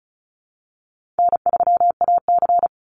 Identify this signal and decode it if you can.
Morse code, heard as a keyed tone, about 34 wpm, 700 Hz. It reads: N3AC
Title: Call Signs Repeated 2x - 35wpm